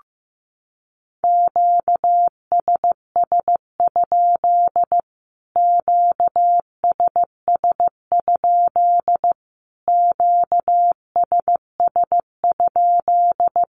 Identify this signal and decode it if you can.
Morse code, heard as a keyed tone, about 15 wpm, 700 Hz. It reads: QSS? QSS? QSS?
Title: Q-Codes Repeated 3x 15wpm